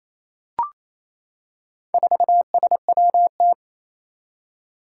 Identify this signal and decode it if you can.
Morse code, heard as a keyed tone, about 27 wpm, 700 Hz. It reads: 4SWT
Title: Four Challenging Letter-Number Combinations 28wpm